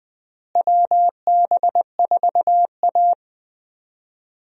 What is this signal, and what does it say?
Morse code, heard as a keyed tone, about 20 wpm, 700 Hz. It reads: WB4A
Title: USA Call Signs - 4 Characters - Repeated 3x - 20wpm